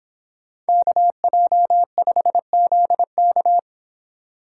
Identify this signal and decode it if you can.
Morse code, heard as a keyed tone, about 24 wpm, 700 Hz. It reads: KJ5ZK